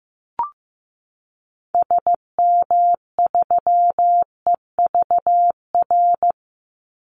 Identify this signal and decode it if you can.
Morse code, heard as a keyed tone, about 15 wpm, 700 Hz. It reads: SM3EVR
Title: Call Signs Repeated 2x - 15wpm